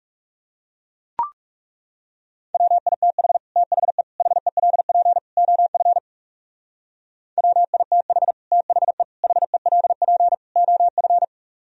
Morse code, heard as a keyed tone, about 36 wpm, 700 Hz. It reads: WITH THE HELP OF WITH THE HELP OF
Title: Speed-Racing - Sets of 4 Words 30wpm